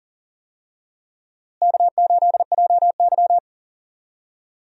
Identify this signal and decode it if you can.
Morse code, heard as a keyed tone, about 37 wpm, 700 Hz. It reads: K8JY
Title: USA Call Signs - 4 Characters - Repeated 2x - 40wpm